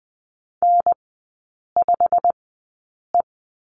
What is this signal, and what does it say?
Morse code, heard as a keyed tone, about 20 wpm, spaced as for 8 wpm, 700 Hz. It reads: N5E
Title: 107 Lesson 18e - Letter m - Three Characters 20wpm